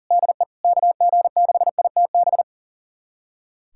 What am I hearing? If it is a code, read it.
Morse code, heard as a keyed tone, about 40 wpm, 700 Hz. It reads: DE KG6ITB
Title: USA Callsign DE Callsign 40wpm